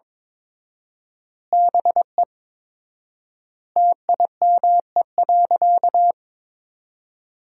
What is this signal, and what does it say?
Morse code, heard as a keyed tone, about 22 wpm, 700 Hz. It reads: BE TIME.
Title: 4x Word-Spacing - Sentences from Top 200 Words - 22wpm